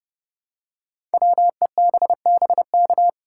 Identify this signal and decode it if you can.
Morse code, heard as a keyed tone, about 30 wpm, 700 Hz. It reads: WEBBK
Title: Code Groups - 5 Letter:Numbers - 30wpm